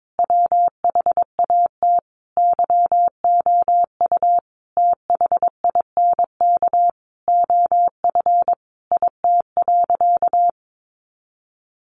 Morse code, heard as a keyed tone, about 22 wpm, 700 Hz. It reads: WHAT YOU THINK OF IT.
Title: Sentences from Top 200 Words 22wpm